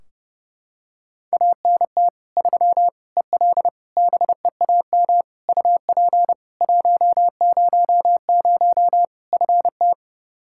Morse code, heard as a keyed tone, about 30 wpm, 700 Hz. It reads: ANT 3 EL BEAM UP 100 FT
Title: K1USN SST QSO Practice 30wpm